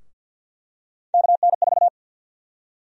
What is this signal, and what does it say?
Morse code, heard as a keyed tone, about 48 wpm, 700 Hz. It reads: KN4
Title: USA Call Sign Prefixes - Repeated 2x - 50wpm